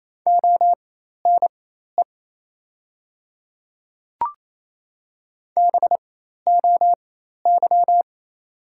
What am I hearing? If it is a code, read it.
Morse code, heard as a keyed tone, about 28 wpm, spaced as for 13 wpm, 700 Hz. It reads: ONE BOY
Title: 4x Character Spacing - Top 300 Words - 28wpm